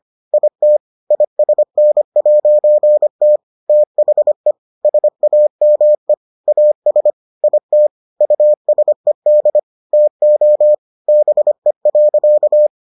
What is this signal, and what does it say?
Morse code, heard as a keyed tone, about 25 wpm, 600 Hz. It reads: IT ISN'T THE SAME AS IT USED TO BE.